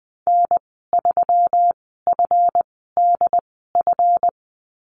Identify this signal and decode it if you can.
Morse code, heard as a keyed tone, about 20 wpm, spaced as for 14 wpm, 700 Hz. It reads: N3FDF
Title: 140 Lesson 23g - Letter f - Call Signs 20wpm